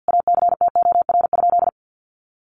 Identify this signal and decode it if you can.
Morse code, heard as a keyed tone, about 50 wpm, 700 Hz. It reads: ACTOR?
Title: Sentences from Top 1500 Words 50wpm